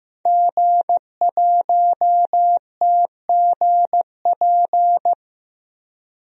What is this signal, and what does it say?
Morse code, heard as a keyed tone, about 15 wpm, 700 Hz. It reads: G1TGP
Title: DXCC-100 Call Signs - 5 Characters - 15wpm